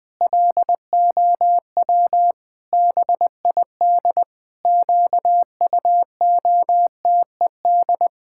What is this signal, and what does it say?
Morse code, heard as a keyed tone, about 20 wpm, 700 Hz. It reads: LOW BID QUOTED
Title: Pangram Challenge v2 - 20wpm